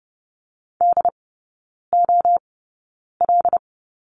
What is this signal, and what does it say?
Morse code, heard as a keyed tone, about 30 wpm, spaced as for 9 wpm, 700 Hz. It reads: DOL
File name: course / 30:10wpm / 087 Lesson 15e - Number 5 - Three Characters 30wpm